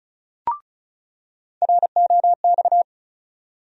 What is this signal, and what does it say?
Morse code, heard as a keyed tone, about 34 wpm, 700 Hz. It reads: ROX
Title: USA Call Sign Postfixes - 3 Letters 35wpm